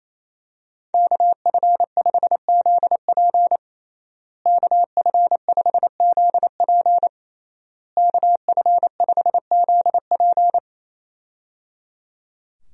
Morse code, heard as a keyed tone, about 27 wpm, 700 Hz. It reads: KF5ZP KF5ZP KF5ZP
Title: USA Call Signs - 5 Characters - Repeated 3x - 28wpm